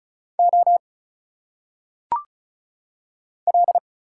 Morse code, heard as a keyed tone, about 36 wpm, 700 Hz. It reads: O L
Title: Single Letters - Phonetic Pronunciation 35wpm